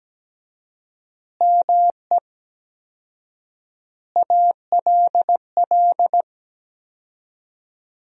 Morse code, heard as a keyed tone, about 17 wpm, 700 Hz. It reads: ME ALL
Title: 4x Word-Spacing - Sentences from Top 200 Words - 17wpm